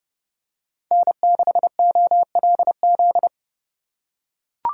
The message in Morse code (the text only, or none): N6OLZ